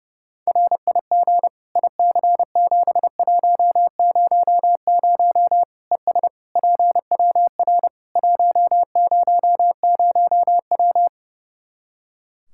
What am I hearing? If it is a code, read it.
Morse code, heard as a keyed tone, about 32 wpm, 700 Hz. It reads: RIG IC7100 ES PWR 100W
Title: Practice QSO - Exchange 2 - Info - 30wpm